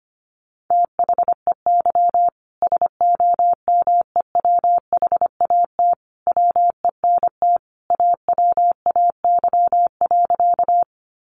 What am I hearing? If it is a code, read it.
Morse code, heard as a keyed tone, about 25 wpm, 700 Hz. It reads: THEY SOMEWHAT WENT AWAY.
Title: Sentences from Top 1300 Words 25wpm